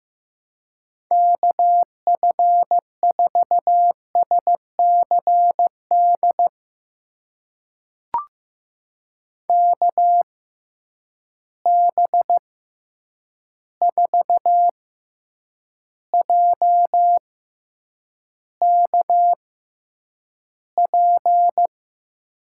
Morse code, heard as a keyed tone, about 15 wpm, spaced as for 8 wpm, 700 Hz. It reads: KF4SCD KB4JKP
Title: USA Call Signs - 6 Characters - 6x Character Spacing - 15wpm